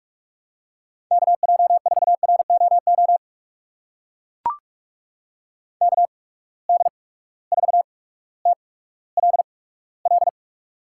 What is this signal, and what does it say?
Morse code, heard as a keyed tone, about 44 wpm, spaced as for 19 wpm, 700 Hz. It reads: KJ4ROO KD4TLL
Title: USA Call Signs - 6 Characters - 8x Character Spacing - 45wpm